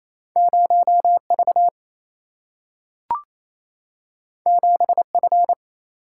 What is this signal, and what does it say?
Morse code, heard as a keyed tone, about 28 wpm, 700 Hz. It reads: 0 V 7 F
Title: Single-Letter Single-Number 28wpm